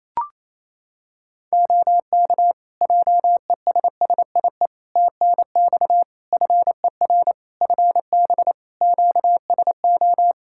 Morse code, heard as a keyed tone, about 28 wpm, 700 Hz. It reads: OK JESSIE TNX FER FB QSO
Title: Practice QSO - Exchange 4 - Ending - 28wpm